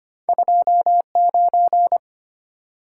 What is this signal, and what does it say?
Morse code, heard as a keyed tone, about 25 wpm, 700 Hz. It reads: 2 9
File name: Two Letter-Number Combinations - 25wpm